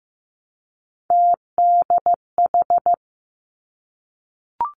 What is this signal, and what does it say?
Morse code, heard as a keyed tone, about 15 wpm, 700 Hz. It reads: TDH
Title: Three Challenging Letter-Number Combinations 15wpm